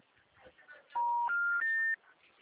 Special information tone.
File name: out- -2014- - - .19158